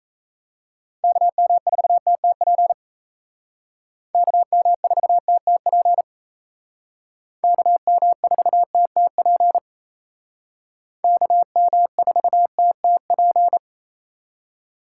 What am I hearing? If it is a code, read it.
Morse code, heard as a keyed tone, about 35 wpm, 700 Hz. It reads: KM4TTP KM4TTP KM4TTP KM4TTP